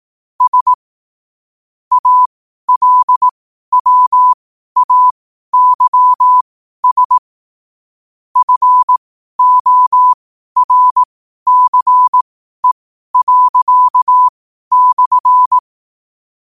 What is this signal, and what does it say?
Morse code, heard as a keyed tone, about 18 wpm, spaced as for 12 wpm, 1000 Hz. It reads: S ALWAYS FORCE./
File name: qotdfiles / 12wpm